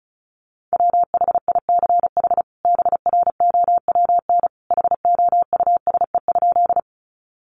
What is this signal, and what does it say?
Morse code, heard as a keyed tone, about 34 wpm, 700 Hz. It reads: WHICH BROWN HOUSE?
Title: Sentences from Top 1200 Words 35wpm